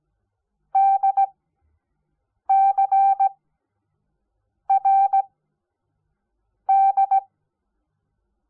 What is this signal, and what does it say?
Morse code, heard as a keyed tone, about 17 wpm, 780 Hz. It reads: D C R D
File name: Onlinelehrgang / morsen / DJ4UF Morsen05